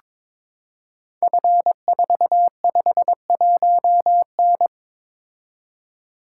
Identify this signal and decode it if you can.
Morse code, heard as a keyed tone, about 22 wpm, 700 Hz. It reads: F451N
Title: Code Groups - 5 Letter:Numbers - 22wpm